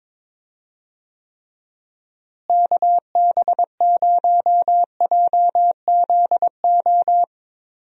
Morse code, heard as a keyed tone, about 21 wpm, 700 Hz. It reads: KB0JZO
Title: USA Call Signs - 6 Characters - Repeated 2x - 22wpm